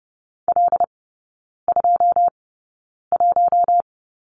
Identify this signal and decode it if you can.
Morse code, heard as a keyed tone, about 30 wpm, spaced as for 9 wpm, 700 Hz. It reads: L21
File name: 081 Lesson 14e - Number 2 - Three Characters 30wpm